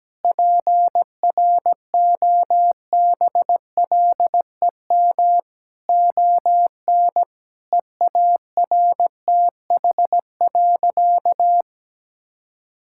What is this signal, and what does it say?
Morse code, heard as a keyed tone, about 17 wpm, 700 Hz. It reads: PROBLEM ON EARTH.
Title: Sentences from Top 1000 Words - Review Part 6 - 17wpm